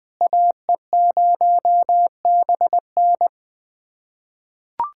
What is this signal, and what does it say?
Morse code, heard as a keyed tone, about 21 wpm, 700 Hz. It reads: AE0BN
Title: USA Callsign DE Callsign - 5 Characters - 20wpm